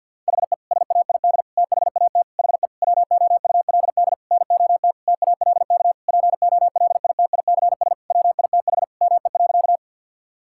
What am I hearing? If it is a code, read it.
Morse code, heard as a keyed tone, about 52 wpm, 700 Hz. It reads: HE SAID THAT HE WOULD NOT TALK POLITICS WITH ME.
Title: Sentences from Top 1000 Words - Review Part 6 - 50wpm